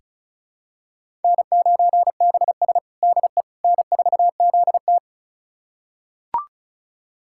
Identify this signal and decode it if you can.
Morse code, heard as a keyed tone, about 34 wpm, 700 Hz. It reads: N9BS DE N4ZT
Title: USA Callsign DE Callsign - 4 Characters - 35wpm